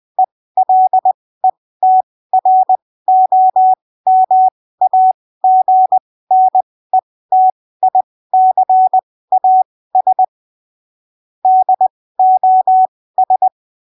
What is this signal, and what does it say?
Morse code, heard as a keyed tone, about 20 wpm, spaced as for 15 wpm, 750 Hz. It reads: ELETROMAGNETICAS DOS
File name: arquivos / texto / texto 13 20ppm